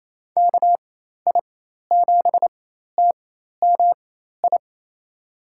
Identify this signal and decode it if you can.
Morse code, heard as a keyed tone, about 28 wpm, spaced as for 13 wpm, 700 Hz. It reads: KI7TMI
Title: USA Call Signs - 6 Characters - 4x Character Spacing - 28wpm